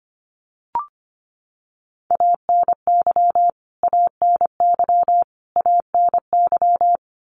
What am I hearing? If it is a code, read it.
Morse code, heard as a keyed tone, about 25 wpm, 700 Hz. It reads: ANY ANY ANY
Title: Top 300 Words Repeated 25wpm